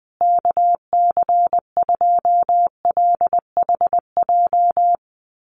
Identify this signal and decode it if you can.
Morse code, heard as a keyed tone, about 20 wpm, 700 Hz. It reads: KC2LHJ
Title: USA Call Signs Repeated 2x 20wpm